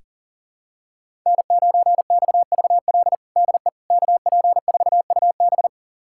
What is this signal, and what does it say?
Morse code, heard as a keyed tone, about 40 wpm, 700 Hz. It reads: N9XVL DE KP4UB
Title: USA Callsign DE Callsign 40wpm